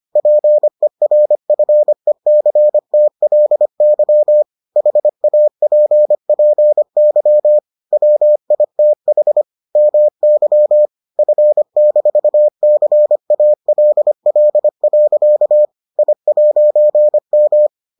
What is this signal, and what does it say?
Morse code, heard as a keyed tone, about 25 wpm, 600 Hz. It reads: PERFECTLY HAPPY WITH MY F-CALL. I'M